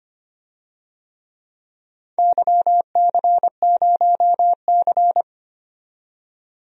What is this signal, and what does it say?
Morse code, heard as a keyed tone, about 25 wpm, 700 Hz. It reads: YC0C
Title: DXCC-100 Call Signs - 4 Characters - 25wpm